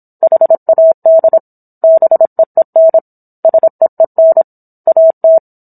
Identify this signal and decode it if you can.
Morse code, heard as a keyed tone, about 27 wpm, 650 Hz. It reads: HAD BEEN SEEN AT
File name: Book One / 17-Chapter 17 - 26 WPM